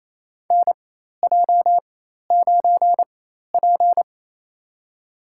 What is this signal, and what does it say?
Morse code, heard as a keyed tone, about 28 wpm, spaced as for 13 wpm, 700 Hz. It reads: NJ9P
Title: USA Call Signs - 4x Character Spacing - 28wpm